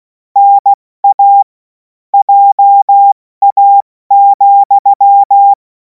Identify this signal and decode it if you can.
Morse code, heard as a keyed tone, about 16 wpm, 800 Hz. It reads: NA JA,
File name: Morsekurs / Witze / Witz 11. August 2026 70 F80